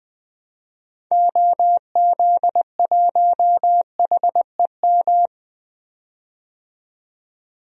Morse code, heard as a keyed tone, about 20 wpm, 700 Hz. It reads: OZ1HEM
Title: DXCC-100 Call Signs - 6 Characters - 20wpm